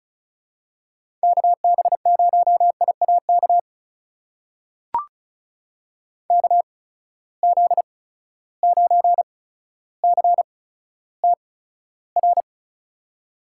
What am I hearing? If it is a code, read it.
Morse code, heard as a keyed tone, about 36 wpm, spaced as for 15 wpm, 700 Hz. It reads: KB0IAK KZ9CTR
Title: USA Call Signs - 6 Characters - 8x Character Spacing - 35wpm